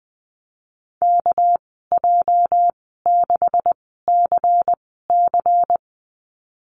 Morse code, heard as a keyed tone, about 20 wpm, spaced as for 15 wpm, 700 Hz. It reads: KJ6CC
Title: 238 Lesson 38g - Letter q - Call Signs 20wpm